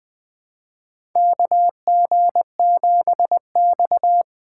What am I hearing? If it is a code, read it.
Morse code, heard as a keyed tone, about 20 wpm, 700 Hz. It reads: KG7X
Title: USA Call Signs - 4 Characters - 8x Character Spacing - 20wpm